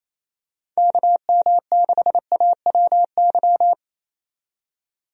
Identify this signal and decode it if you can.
Morse code, heard as a keyed tone, about 28 wpm, 700 Hz. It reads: KM6AWY
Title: USA Call Signs - 6 Characters - 28wpm